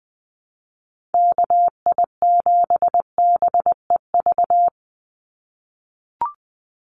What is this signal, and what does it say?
Morse code, heard as a keyed tone, about 20 wpm, 700 Hz. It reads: KI7BEV
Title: USA Call Signs 20wpm